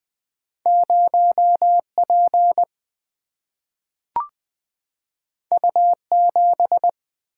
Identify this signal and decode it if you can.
Morse code, heard as a keyed tone, about 20 wpm, 700 Hz. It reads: 0P U7
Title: Single-Letter Single-Number 20wpm